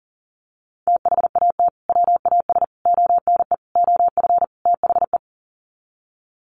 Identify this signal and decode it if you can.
Morse code, heard as a keyed tone, about 40 wpm, 700 Hz. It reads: THAT WAS ONE OF THE
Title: Sets of 5 Words 40wpm